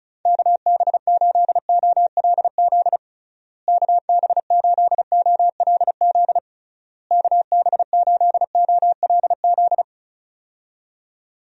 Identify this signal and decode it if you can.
Morse code, heard as a keyed tone, about 34 wpm, 700 Hz. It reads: KB8OLZ KB8OLZ KB8OLZ
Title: USA Call Signs - 6 Characters - Repeated 3x - 35wpm